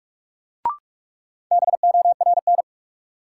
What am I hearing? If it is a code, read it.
Morse code, heard as a keyed tone, about 46 wpm, 700 Hz. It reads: BORN
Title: Sentences from Top 1000 Words - New Word Drill - 45wpm